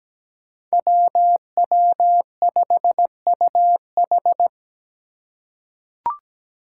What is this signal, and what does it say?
Morse code, heard as a keyed tone, about 17 wpm, 700 Hz. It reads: WW5UH
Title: Five Challenging Letter-Number Combinations 17wpm